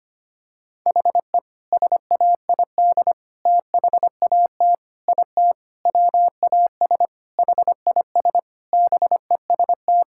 Morse code, heard as a keyed tone, about 25 wpm, 700 Hz. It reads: HE SAID THAT IT WAS HIS BEST
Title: Sentences from Top 1600 Words 25wpm